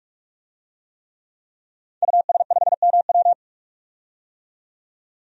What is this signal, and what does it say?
Morse code, heard as a keyed tone, about 42 wpm, 700 Hz. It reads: US5MW
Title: DXCC-100 Call Signs - 5 Characters - 45wpm